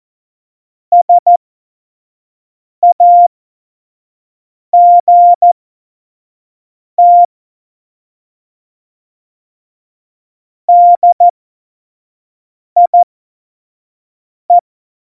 Morse code, heard as a keyed tone, about 14 wpm, spaced as for 5 wpm, 700 Hz. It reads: SAGT DIE
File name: Morsekurs / Witze / Witz 16. Oktober 2018, 25 F70